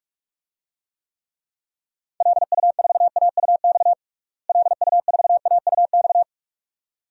Morse code, heard as a keyed tone, about 46 wpm, 700 Hz. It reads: LU4AUX LU4AUX
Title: DXCC-100 Call Signs - 6 Characters - Repeated 2x - 45wpm